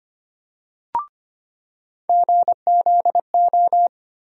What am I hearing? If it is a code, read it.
Morse code, heard as a keyed tone, about 25 wpm, 700 Hz. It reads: GZO
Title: USA Call Sign Postfixes - 3 Letters 25wpm